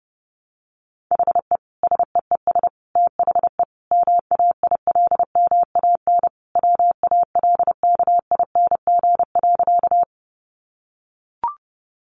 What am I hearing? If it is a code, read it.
Morse code, heard as a keyed tone, about 30 wpm, 700 Hz. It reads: HE SEES THE MAILMAN WALKING.
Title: Additional Easy Phrases 30wpm